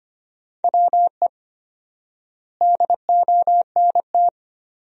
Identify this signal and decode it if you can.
Morse code, heard as a keyed tone, about 25 wpm, 700 Hz. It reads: WE DONT
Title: 4x Word-Spacing - Sentences from Top 400 Words - 25wpm